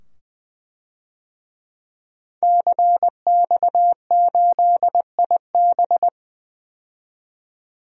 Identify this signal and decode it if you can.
Morse code, heard as a keyed tone, about 20 wpm, 700 Hz. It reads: CX8IB
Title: DXCC-100 Call Signs - 5 Characters - 20wpm